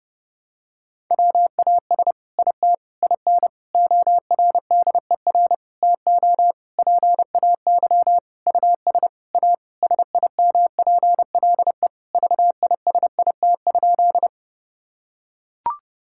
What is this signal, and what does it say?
Morse code, heard as a keyed tone, about 30 wpm, 700 Hz. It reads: WAS IT IN ORDER TO PAY US A SIMPLE VISIT?
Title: Sentences from Top 1000 Words - Review Part 6 - 30wpm